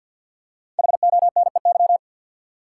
Morse code, heard as a keyed tone, about 50 wpm, 700 Hz. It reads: HONEY
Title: Five Letter Words - 50wpm